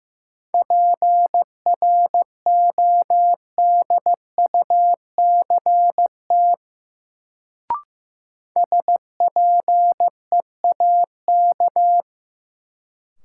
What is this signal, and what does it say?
Morse code, heard as a keyed tone, about 15 wpm, 700 Hz. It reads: PRODUCT SPEAK
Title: Sentences from Top 1000 Words - Review Part 2 - New Word Drill - 15wpm